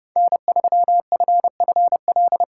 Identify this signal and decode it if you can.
Morse code, heard as a keyed tone, about 30 wpm, 700 Hz. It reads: N3FFL